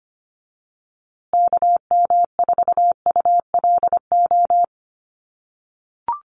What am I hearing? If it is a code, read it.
Morse code, heard as a keyed tone, about 25 wpm, 700 Hz. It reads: KM4ULO
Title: USA Call Signs Repeated 3x 25wpm